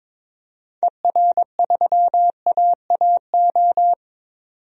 Morse code, heard as a keyed tone, about 22 wpm, 700 Hz. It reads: ER3AAO
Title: DXCC-100 Call Signs - 6 Characters - 22wpm